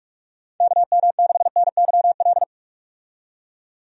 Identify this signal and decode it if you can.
Morse code, heard as a keyed tone, about 46 wpm, 700 Hz. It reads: KM6NYL
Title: USA Call Signs - 6 Characters - 45wpm